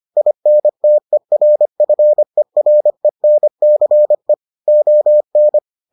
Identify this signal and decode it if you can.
Morse code, heard as a keyed tone, about 25 wpm, 600 Hz. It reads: INTERFERENCE ON